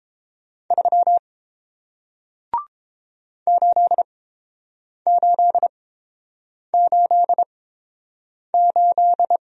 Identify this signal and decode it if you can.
Morse code, heard as a keyed tone, about 30 wpm, spaced as for 7 wpm, 700 Hz. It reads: 3 8888